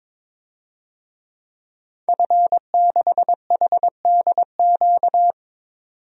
Morse code, heard as a keyed tone, about 22 wpm, 700 Hz. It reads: F6HDQ
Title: DXCC-100 Call Signs - 5 Characters - 22wpm